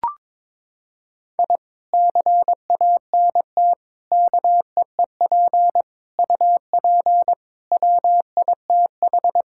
Morse code, heard as a keyed tone, about 22 wpm, 700 Hz. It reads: I CANT KEEP UP WITH
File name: practice / 22wpm / Sentences with 2-Word Phrases - 22wpm